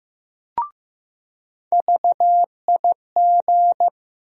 Morse code, heard as a keyed tone, about 16 wpm, 700 Hz. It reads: VIG
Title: USA Call Sign Postfixes - 3 Letters 15wpm